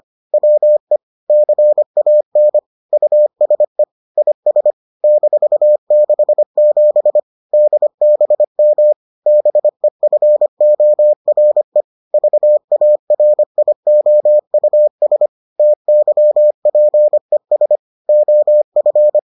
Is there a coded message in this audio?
Yes: Morse code from 0.3 to 19.2 s, heard as a keyed tone, about 25 wpm, 600 Hz. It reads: WE CAN USE IS -67 DBM BEFORE VARIOUS TYPES OF